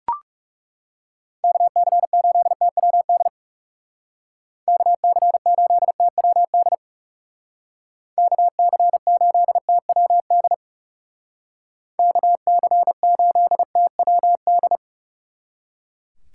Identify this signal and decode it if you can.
Morse code, heard as a keyed tone, about 36 wpm, 700 Hz. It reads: KC8TWD KC8TWD KC8TWD KC8TWD
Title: Speed Racing - USA 2x3 Call Signs 30wpm